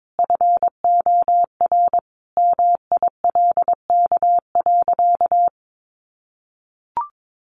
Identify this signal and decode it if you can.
Morse code, heard as a keyed tone, about 22 wpm, 700 Hz. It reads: FOR MILK.
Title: Sentences from Top 1500 Words 22wpm